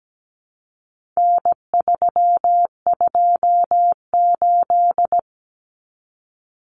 Morse code, heard as a keyed tone, about 17 wpm, 700 Hz. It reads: N328
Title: Common Cut Numbers - 4-Digit 17wpm